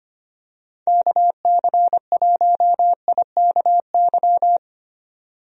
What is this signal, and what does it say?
Morse code, heard as a keyed tone, about 25 wpm, 700 Hz. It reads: KC1IKY
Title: USA Call Signs - 6 Characters - 25wpm